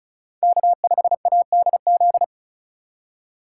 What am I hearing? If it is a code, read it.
Morse code, heard as a keyed tone, about 36 wpm, 700 Hz. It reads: K5ADZ
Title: USA Call Signs - 2x Character Spacing - 35wpm